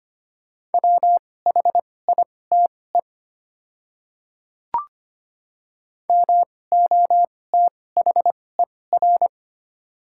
Morse code, heard as a keyed tone, about 25 wpm, spaced as for 18 wpm, 690 Hz. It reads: WHITE MOTHER
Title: 2x Character Spacing - Top 300 Words - 25wpm